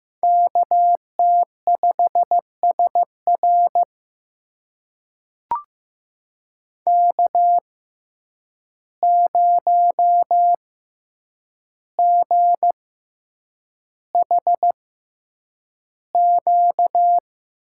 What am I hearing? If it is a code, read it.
Morse code, heard as a keyed tone, about 15 wpm, spaced as for 8 wpm, 700 Hz. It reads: KT5SR K0GHQ
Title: USA Call Signs - 5 Characters - 6x Character Spacing - 15wpm